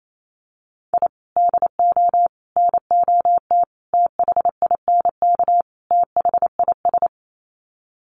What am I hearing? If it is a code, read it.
Morse code, heard as a keyed tone, about 28 wpm, 700 Hz. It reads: I DO NOT THINK THIS
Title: Sets of 5 Words Encore 28wpm